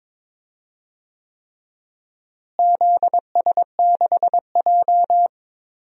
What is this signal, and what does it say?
Morse code, heard as a keyed tone, about 22 wpm, 700 Hz. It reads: ZS6J
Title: DXCC-100 Call Signs - 4 Characters - Repeated 3x - 22wpm